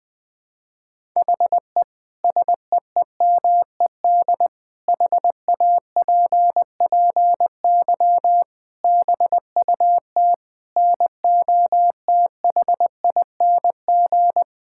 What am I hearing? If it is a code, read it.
Morse code, heard as a keyed tone, about 20 wpm, 700 Hz. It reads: HE SEEMED HAPPY BUT NOTHING
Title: Sentences with 8-Word Phrases - Spoken Beginning - 20wpm